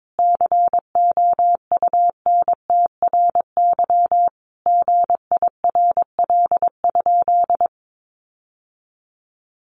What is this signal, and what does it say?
Morse code, heard as a keyed tone, about 22 wpm, 700 Hz. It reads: COUNTRY GIRL?
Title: Sentences from Top 1500 Words 22wpm